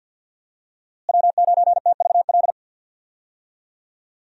Morse code, heard as a keyed tone, about 53 wpm, 700 Hz. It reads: W9TVL
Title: USA Call Signs - 5 Characters - Repeated 2x - 50wpm